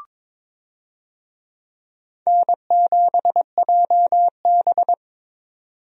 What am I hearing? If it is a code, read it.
Morse code, heard as a keyed tone, about 22 wpm, 700 Hz. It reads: N7JB
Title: USA Call Signs - 4 Characters - 22wpm